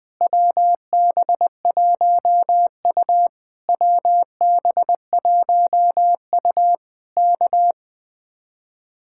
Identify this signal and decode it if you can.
Morse code, heard as a keyed tone, about 20 wpm, 700 Hz. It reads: WB1U WB1U K
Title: Speed-Racing Common QSO Phrases Encore - 17wpm